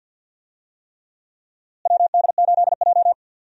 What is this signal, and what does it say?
Morse code, heard as a keyed tone, about 50 wpm, 700 Hz. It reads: WD8J
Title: USA Call Signs - 4 Characters - Repeated 3x - 50wpm